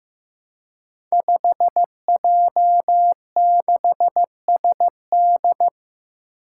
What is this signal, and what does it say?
Morse code, heard as a keyed tone, about 15 wpm, 700 Hz. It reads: 5J6SD
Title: Five Challenging Letter-Number Combinations 15wpm